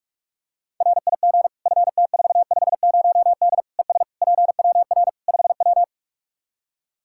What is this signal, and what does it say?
Morse code, heard as a keyed tone, about 46 wpm, 700 Hz. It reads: RIG FT450D ES PWR 5W
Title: Practice QSO - Exchange 2 - Info - 45wpm